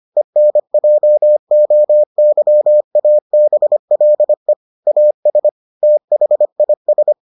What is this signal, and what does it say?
Morse code, heard as a keyed tone, about 25 wpm, 600 Hz. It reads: ENJOYABLE AS THIS